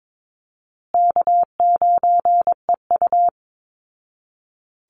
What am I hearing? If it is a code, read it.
Morse code, heard as a keyed tone, about 22 wpm, 700 Hz. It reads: K9EU
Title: Call Signs Repeated 3x - 22wpm